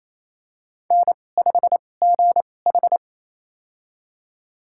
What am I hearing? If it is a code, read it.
Morse code, heard as a keyed tone, about 28 wpm, spaced as for 20 wpm, 700 Hz. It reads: N5GH
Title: USA Call Signs - 4 Characters - 2x Character Spacing - 28wpm